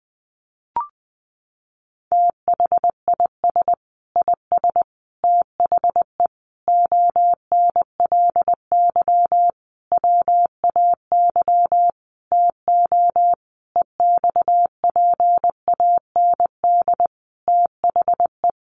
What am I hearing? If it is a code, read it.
Morse code, heard as a keyed tone, about 19 wpm, 700 Hz. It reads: THIS IS THE ONLY WAY TO EXPAND THE
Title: Sentences from Top 1400 Words 20wpm